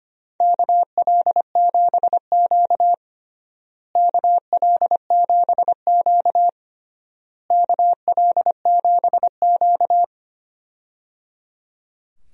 Morse code, heard as a keyed tone, about 25 wpm, 700 Hz. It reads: KL7Q KL7Q KL7Q
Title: DXCC-100 Call Signs - 4 Characters - Repeated 3x - 25wpm